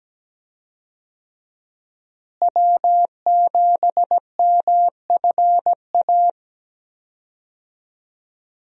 Morse code, heard as a keyed tone, about 17 wpm, 700 Hz. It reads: W7MFA
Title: USA Call Signs - 5 Characters - Repeated 2x - 17wpm